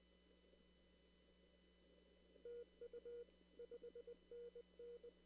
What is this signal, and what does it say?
Morse code, heard as a keyed tone, about 20 wpm, 480 Hz. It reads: TU 5NN
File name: CW / kt5j wpx cw-op-k5pi